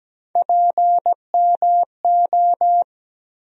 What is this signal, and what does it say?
Morse code, heard as a keyed tone, about 17 wpm, 700 Hz. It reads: PMO
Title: USA Call Sign Postfixes - 3 Letters 17wpm